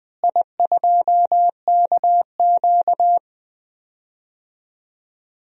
Morse code, heard as a keyed tone, about 20 wpm, 700 Hz. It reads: I2KQ
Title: DXCC-100 Call Signs - 4 Characters - Repeated 3x - 20wpm